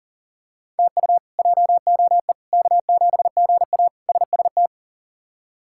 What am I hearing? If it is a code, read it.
Morse code, heard as a keyed tone, about 40 wpm, 700 Hz. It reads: TU JOE K7GA SST
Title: K1USN SST QSO Practice - 40wpm